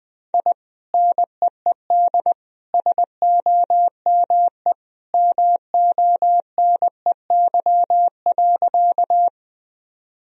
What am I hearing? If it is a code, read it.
Morse code, heard as a keyed tone, about 21 wpm, 700 Hz. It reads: I NEED SOME MONEY.
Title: Sentences from Top 1000 Words - Review Part 1 - 20wpm